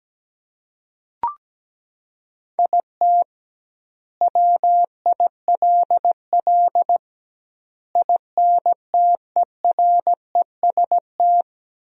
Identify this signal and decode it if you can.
Morse code, heard as a keyed tone, about 17 wpm, 700 Hz. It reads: IT WILL INTEREST
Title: 2x Word-Spacing - Sentences from Top 400 Words - 17wpm